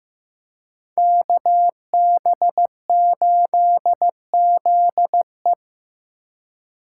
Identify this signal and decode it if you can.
Morse code, heard as a keyed tone, about 18 wpm, 700 Hz. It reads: KB8ZE
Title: Speed Racing - USA 2x2 Call Signs 15wpm